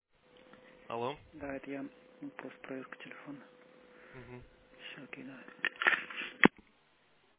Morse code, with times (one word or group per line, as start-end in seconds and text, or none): none